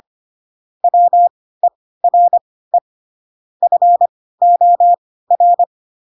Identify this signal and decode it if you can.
Morse code, heard as a keyed tone, about 25 wpm, spaced as for 16 wpm, 700 Hz. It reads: WERE FOR
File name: Pinocchio / Pinocchio ch7 10 25 ik1hge